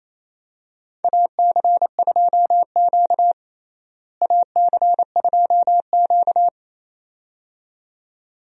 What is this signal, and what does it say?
Morse code, heard as a keyed tone, about 27 wpm, 700 Hz. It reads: AC2Q AC2Q
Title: USA Call Signs - 4 Characters - Repeated 2x - 28wpm